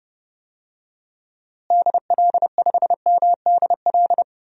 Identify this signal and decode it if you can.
Morse code, heard as a keyed tone, about 31 wpm, 700 Hz. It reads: DL5MDL
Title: DXCC-100 Call Signs - 6 Characters - Repeated 2x - 30wpm